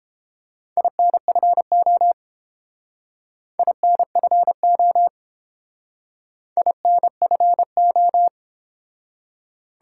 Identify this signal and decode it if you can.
Morse code, heard as a keyed tone, about 29 wpm, 700 Hz. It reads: INFO INFO INFO